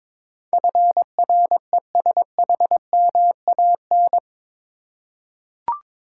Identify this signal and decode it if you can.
Morse code, heard as a keyed tone, about 21 wpm, 700 Hz. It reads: FRESHMAN
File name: practice / 22wpm / Eight Letter Words - 22wpm